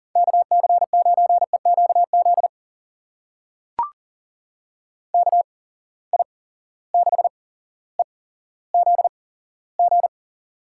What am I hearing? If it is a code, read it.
Morse code, heard as a keyed tone, about 42 wpm, spaced as for 17 wpm, 700 Hz. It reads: KC9EQZ KI6EZG
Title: USA Call Signs - 6 Characters - 8x Character Spacing - 40wpm